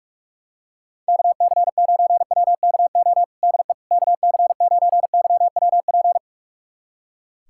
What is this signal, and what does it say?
Morse code, heard as a keyed tone, about 45 wpm, 700 Hz. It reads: KC9WKO DE KC9YWP
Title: USA Callsign DE Callsign 45wpm